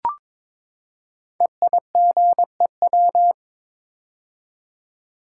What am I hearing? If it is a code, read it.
Morse code, heard as a keyed tone, about 22 wpm, 700 Hz. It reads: EIGEW
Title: Code Groups - 5 Letters - 22wpm